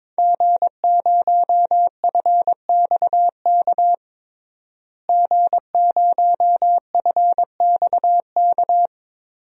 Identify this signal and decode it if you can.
Morse code, heard as a keyed tone, about 22 wpm, 700 Hz. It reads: G0FXK G0FXK
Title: DXCC-100 Call Signs - 5 Characters - Repeated 2x - 22wpm